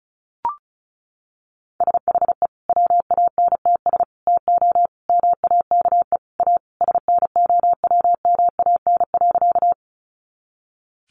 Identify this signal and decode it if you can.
Morse code, heard as a keyed tone, about 34 wpm, 700 Hz. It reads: SHE WANTS TO MAKE A SNOWMAN.
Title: Easy Phrases 35wpm